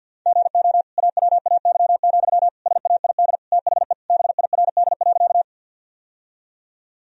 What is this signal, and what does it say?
Morse code, heard as a keyed tone, about 46 wpm, 700 Hz. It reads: GO AWAY, SAID THE BIRD.
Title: Sentences from Top 1000 Words - Review Part 2 - 50wpm